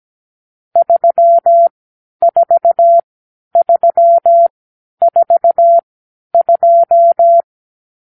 Morse code, heard as a keyed tone, about 17 wpm, spaced as for 10 wpm, 680 Hz. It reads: 34342
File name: k7qo / Lesson 50 (1-6)